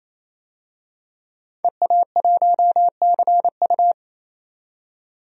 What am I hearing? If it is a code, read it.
Morse code, heard as a keyed tone, about 28 wpm, 700 Hz. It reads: EA1CU
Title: DXCC-100 Call Signs - 5 Characters - 28wpm